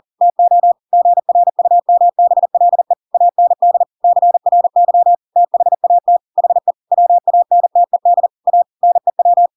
Morse code, heard as a keyed tone, about 40 wpm, 700 Hz. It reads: TO GRUMBLE AND CRY THAT HE WANTED A NEW